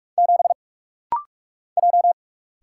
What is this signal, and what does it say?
Morse code, heard as a keyed tone, about 46 wpm, 700 Hz. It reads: 7 J
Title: Single Letter-Number - Rapid-Fire 45wpm